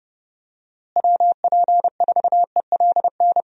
Morse code, heard as a keyed tone, about 30 wpm, 700 Hz. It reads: WP4ELN
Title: USA Call Signs - 6 Characters - Repeated 3x - 30wpm